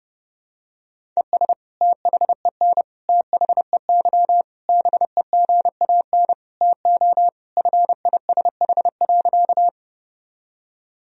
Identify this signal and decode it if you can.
Morse code, heard as a keyed tone, about 32 wpm, 700 Hz. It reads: ES THEN THEY BEGAN TO FISH.
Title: Sentences from Top 300 Words and Abbreviations 30wpm